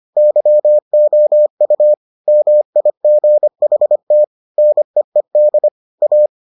Morse code, heard as a keyed tone, about 25 wpm, 600 Hz. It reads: YOU MIGHT NEED A